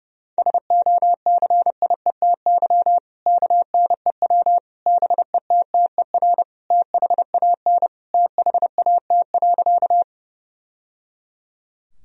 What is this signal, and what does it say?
Morse code, heard as a keyed tone, about 30 wpm, 700 Hz. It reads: SOCIETY KNEW BETTER THAN THAT.
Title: Sentences from Top 1000 Words - Review Part 4 - 30wpm